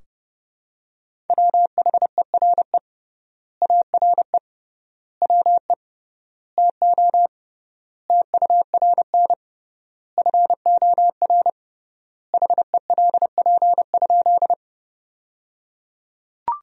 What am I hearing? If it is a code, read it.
Morse code, heard as a keyed tone, about 30 wpm, 700 Hz. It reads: WHERE ARE WE TO TURN FOR HELP?
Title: 3x Word-Spacing - Sentences from Top 200 Words - 30wpm